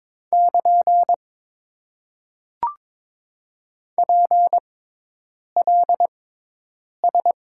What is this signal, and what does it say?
Morse code, heard as a keyed tone, about 22 wpm, spaced as for 8 wpm, 700 Hz. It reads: <KN> PLS
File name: Top 90 QSO Elements - 6x Character Spacing - 22wpm